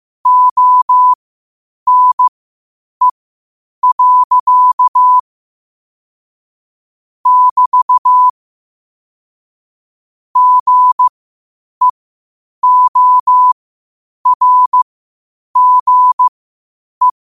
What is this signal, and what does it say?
Morse code, heard as a keyed tone, about 15 wpm, spaced as for 8 wpm, 1000 Hz. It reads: ONE. <BT> GEORGE
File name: 8wpm